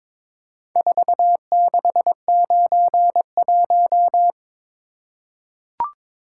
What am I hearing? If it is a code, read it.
Morse code, heard as a keyed tone, about 22 wpm, 700 Hz. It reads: 4691
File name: practice / 22wpm / Four Letter-Number Combinations - 22wpm